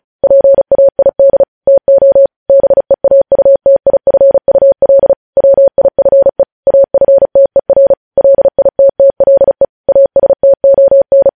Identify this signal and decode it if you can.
Morse code, heard as a keyed tone, about 36 wpm, 550 Hz. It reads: PAID TO BEAUTIFUL WIFE AFTER LITTLE ASTON
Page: News Headlines in Morse Code at 35 WPM